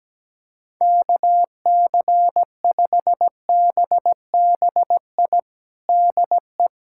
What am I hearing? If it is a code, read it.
Morse code, heard as a keyed tone, about 17 wpm, 700 Hz. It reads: KC5BBI DE
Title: USA Callsign DE Callsign - 6 Characters - 17wpm